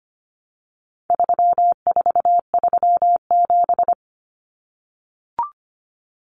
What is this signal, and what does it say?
Morse code, heard as a keyed tone, about 25 wpm, 700 Hz. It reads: 3437
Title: Common Cut Numbers - 4-Digit 25wpm